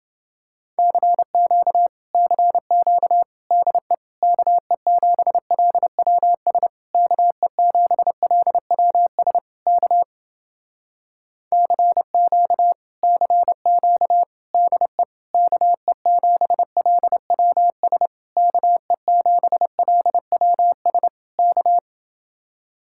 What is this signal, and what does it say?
Morse code, heard as a keyed tone, about 29 wpm, 700 Hz. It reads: CQ CQ DE KE7LWS KE7LWS K CQ CQ DE KE7LWS KE7LWS K